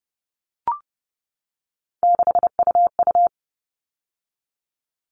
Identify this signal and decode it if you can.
Morse code, heard as a keyed tone, about 30 wpm, 700 Hz. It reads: 6UU
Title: All Cut Numbers - 3-Digit 30wpm